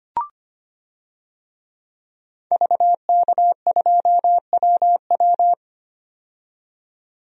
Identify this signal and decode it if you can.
Morse code, heard as a keyed tone, about 25 wpm, 700 Hz. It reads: VK2WW
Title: DXCC-100 Call Signs - 5 Characters - 25wpm